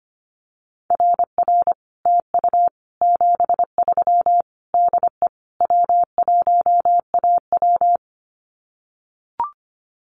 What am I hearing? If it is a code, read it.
Morse code, heard as a keyed tone, about 25 wpm, 700 Hz. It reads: RR TU 73 DE W1AW
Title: Common QSO Phrases Encore 25wpm